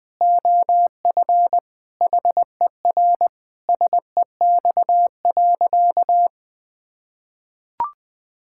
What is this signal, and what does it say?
Morse code, heard as a keyed tone, about 20 wpm, 700 Hz. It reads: OF HER SEX.
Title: Sentences from Top 1600 Words 20wpm